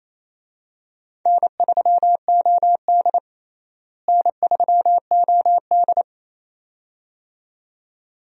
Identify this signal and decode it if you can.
Morse code, heard as a keyed tone, about 28 wpm, 700 Hz. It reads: N3OD N3OD
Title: USA Call Signs - 4 Characters - Repeated 2x - 28wpm